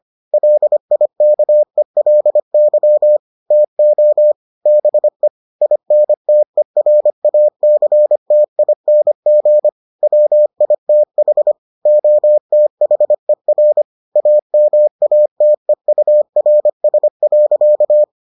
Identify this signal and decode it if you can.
Morse code, heard as a keyed tone, about 25 wpm, 600 Hz. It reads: LIKELY TO BE INTERACTING WITH OTHER AMATEURS.